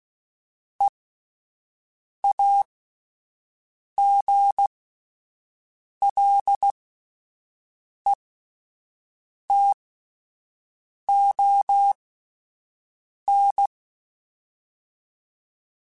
Morse code, heard as a keyed tone, about 16 wpm, spaced as for 5 wpm, 800 Hz. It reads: EAGLETON